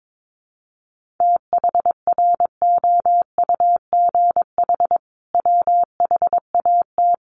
Morse code, heard as a keyed tone, about 22 wpm, 700 Hz. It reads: THROUGH WHAT
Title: Sentences from Top 900 Words 22wpm